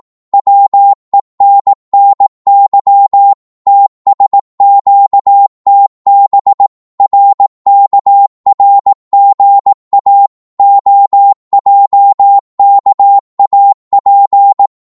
Morse code, heard as a keyed tone, about 18 wpm, 800 Hz. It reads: WENNY TSQTB RKRGA OJKAP
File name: Morse / ltr-18wpm